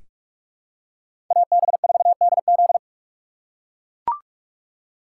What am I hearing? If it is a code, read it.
Morse code, heard as a keyed tone, about 43 wpm, 700 Hz. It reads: AB4DZ
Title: USA Call Signs - 5 Characters - 4x Character Spacing - 45wpm